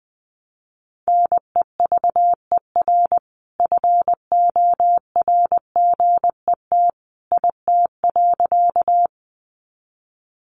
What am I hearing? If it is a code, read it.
Morse code, heard as a keyed tone, about 20 wpm, 700 Hz. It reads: NEVER FORGET IT.
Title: Sentences from Top 1200 Words 20wpm